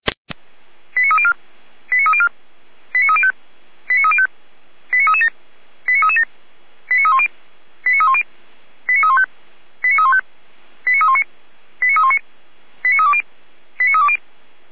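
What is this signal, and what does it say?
Five-tone selective call (ZVEI), heard as quick runs of five tones, tones of about 70 ms. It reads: 89304 89304 89305 89305 89337 89337 89310 89310 89315 89315 89319 89319 89320 89320
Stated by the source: ZVEI five-tone selective call 89270 received by SDR.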